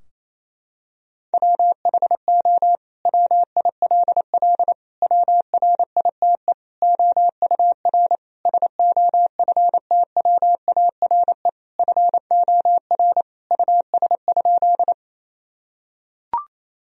Morse code, heard as a keyed tone, about 28 wpm, 700 Hz. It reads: WHO WILL WRITE OUR SOFTWARE FOR US?
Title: Sentences from Top 1000 Words - Review Part 3 - 28wpm